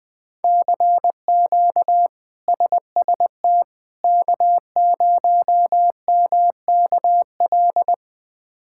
Morse code, heard as a keyed tone, about 20 wpm, 700 Hz. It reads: CQ SST K0MKL
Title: K1USN SST QSO Practice - 20wpm